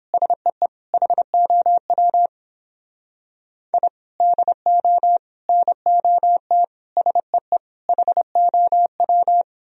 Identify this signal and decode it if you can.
Morse code, heard as a keyed tone, about 29 wpm, 700 Hz. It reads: SEE HOW I DO NOT SEE HOW